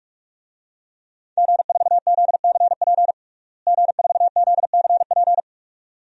Morse code, heard as a keyed tone, about 46 wpm, 700 Hz. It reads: G4ZCP G4ZCP
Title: DXCC-100 Call Signs - 5 Characters - Repeated 2x - 45wpm